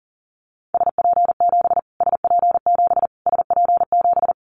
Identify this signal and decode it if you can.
Morse code, heard as a keyed tone, about 40 wpm, 700 Hz. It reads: SP7 SP7 SP7
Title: Callsign Prefixes for Top 25 DXCC - Repeated 3x - 40wpm